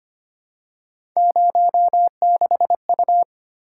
Morse code, heard as a keyed tone, about 25 wpm, 700 Hz. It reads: 06U
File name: Three Letter-Number Combinations - 25wpm